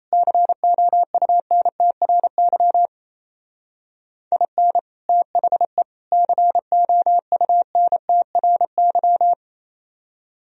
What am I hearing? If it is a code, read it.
Morse code, heard as a keyed tone, about 34 wpm, 700 Hz. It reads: COUNTRY IN THE COUNTRY
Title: Speed-Racing - Sets of 3 Words 28wpm